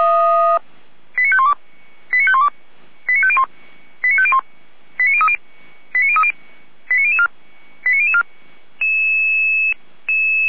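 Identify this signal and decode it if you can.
Five-tone selective call (ZVEI), heard as quick runs of five tones, tones of about 70 ms. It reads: 89512 89512 89501 89501 89030 89030 89004 89004
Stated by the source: Five-tone ZVEI selective call received on SDR, from the feuerwehr-orlamuende.org fire brigade audio server (f85360).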